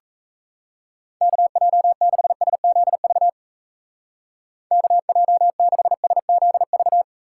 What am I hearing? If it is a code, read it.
Morse code, heard as a keyed tone, about 37 wpm, 700 Hz. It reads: KJ6SZV KJ6SZV